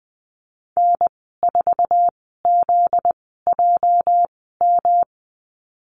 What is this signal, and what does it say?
Morse code, heard as a keyed tone, about 20 wpm, spaced as for 14 wpm, 700 Hz. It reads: N4ZJM
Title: 245 Lesson 39g - Letter z - Call Signs 20wpm